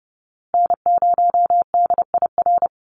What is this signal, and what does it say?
Morse code, heard as a keyed tone, about 30 wpm, 700 Hz. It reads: N0DIR
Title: Call Signs Repeated 2x - 30wpm